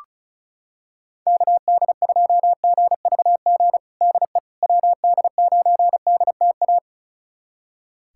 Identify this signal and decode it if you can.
Morse code, heard as a keyed tone, about 36 wpm, 700 Hz. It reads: KD2GVG DE WD9DTA
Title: USA Callsign DE Callsign - 6 Characters - 35wpm